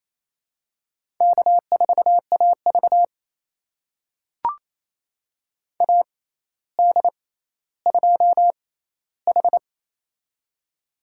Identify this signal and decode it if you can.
Morse code, heard as a keyed tone, about 28 wpm, spaced as for 14 wpm, 700 Hz. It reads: K4AV AD2H
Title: USA Call Signs - 4 Characters - 6x Character Spacing - 28wpm